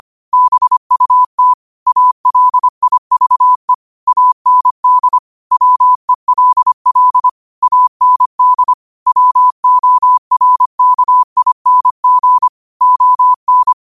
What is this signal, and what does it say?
Morse code, heard as a keyed tone, about 25 wpm, 1000 Hz. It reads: BUT ALIVE AND WELL AND WORKING ON
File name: qotdfiles / 25wpm